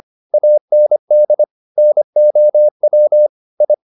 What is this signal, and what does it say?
Morse code, heard as a keyed tone, about 25 wpm, 600 Hz. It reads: AND NOW I